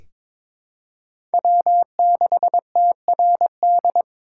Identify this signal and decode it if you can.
Morse code, heard as a keyed tone, about 21 wpm, 700 Hz. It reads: W6TRD
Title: USA Call Signs - 5 Characters - 3x Character Spacing - 22wpm